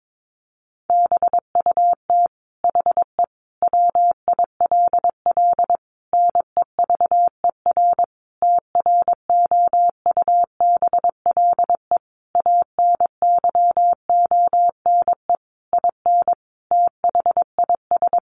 Morse code, heard as a keyed tone, about 22 wpm, 700 Hz. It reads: BUT HE WILL NEVER TROUBLE ANYONE IN THIS
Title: Sentences from Top 1200 Words 22wpm